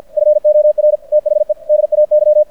Morse code, heard as a keyed tone, about 53 wpm, 600 Hz. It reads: FOR THE DAY
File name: archived / CRSnet